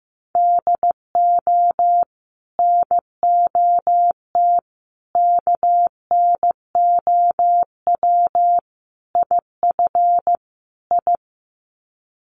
Morse code, heard as a keyed tone, about 15 wpm, 700 Hz. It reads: DO NOT KNOW IF I
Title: Sets of 5 Words 15wpm